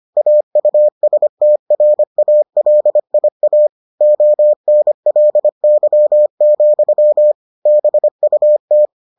Morse code, heard as a keyed tone, about 25 wpm, 600 Hz. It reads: AUSTRALIA ONLY, BUT